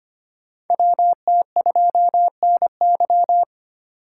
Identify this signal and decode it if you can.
Morse code, heard as a keyed tone, about 25 wpm, 700 Hz. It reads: WT2NY
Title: USA Call Signs - 3x Character Spacing - 25wpm